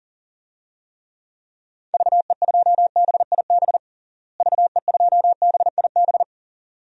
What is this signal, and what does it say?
Morse code, heard as a keyed tone, about 40 wpm, 700 Hz. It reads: VE2BIB VE2BIB
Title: DXCC-100 Call Signs - 6 Characters - Repeated 2x - 40wpm